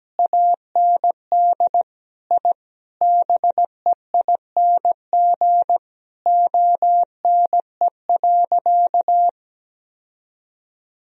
Morse code, heard as a keyed tone, about 17 wpm, 700 Hz. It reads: AND I BEING ONE.
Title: Sentences from Top 1000 Words - Review Part 3 - 17wpm